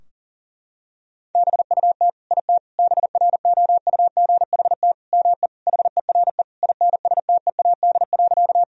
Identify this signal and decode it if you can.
Morse code, heard as a keyed tone, about 40 wpm, 700 Hz. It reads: BUT IT BROUGHT ME HERE INSTEAD.
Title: Sentences from Top 1000 Words - Review Part 4 - 40wpm